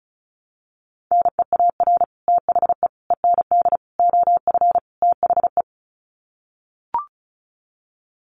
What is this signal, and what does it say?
Morse code, heard as a keyed tone, about 34 wpm, 700 Hz. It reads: NEAR THE END OF THE
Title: Sets of 5 Words Encore 35wpm